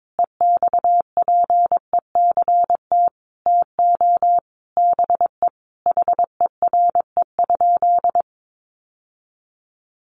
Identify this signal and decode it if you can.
Morse code, heard as a keyed tone, about 22 wpm, 700 Hz. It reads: EXPECT TO BE HERE?
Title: Sentences from Top 1100 Words 22wpm